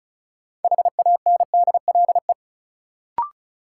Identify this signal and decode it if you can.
Morse code, heard as a keyed tone, about 36 wpm, 700 Hz. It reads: HANDLE
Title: Sentences from Top 1100 Words - New Word Drill - 35wpm